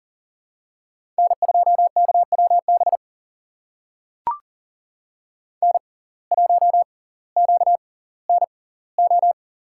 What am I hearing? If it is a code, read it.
Morse code, heard as a keyed tone, about 40 wpm, spaced as for 20 wpm, 700 Hz. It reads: N2KWB N1QNO
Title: USA Call Signs - 6x Character Spacing - 40wpm